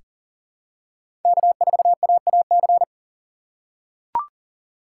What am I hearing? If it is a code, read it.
Morse code, heard as a keyed tone, about 38 wpm, 700 Hz. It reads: K4AAC
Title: K1USN SST QSO Practice - 40wpm